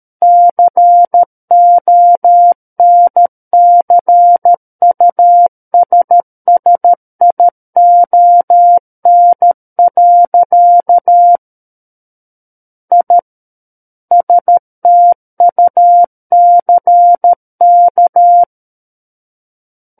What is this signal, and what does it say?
Morse code, heard as a keyed tone, about 13 wpm, 700 Hz. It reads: CONCUSSION. I STUCK
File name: Book One / 04-Chapter 4 - 13 WPM